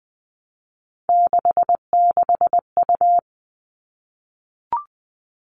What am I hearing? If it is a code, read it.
Morse code, heard as a keyed tone, about 20 wpm, 700 Hz. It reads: 66U
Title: All Cut Numbers - 3-Digit 20wpm